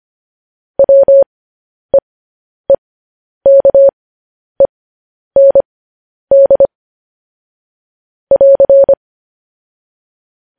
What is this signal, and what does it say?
Morse code, heard as a keyed tone, about 25 wpm, spaced as for 10 wpm, 550 Hz. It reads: WEEKEND <AR>
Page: news headlines in morse code at 10 wpm